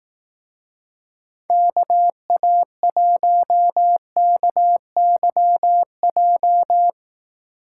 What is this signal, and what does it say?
Morse code, heard as a keyed tone, about 17 wpm, 700 Hz. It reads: KA1KYJ
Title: Speed Racing - USA Call Signs 15wpm